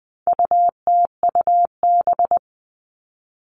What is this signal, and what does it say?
Morse code, heard as a keyed tone, about 20 wpm, 700 Hz. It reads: UTUB
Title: All Cut Numbers - 4-Digit 20wpm